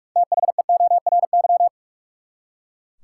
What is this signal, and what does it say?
Morse code, heard as a keyed tone, about 43 wpm, 700 Hz. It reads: THEORY
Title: Six Letter Words - 45wpm